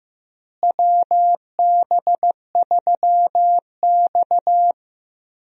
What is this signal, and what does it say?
Morse code, heard as a keyed tone, about 15 wpm, 700 Hz. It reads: WB3X
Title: USA Call Signs - 4 Characters - Repeated 3x - 15wpm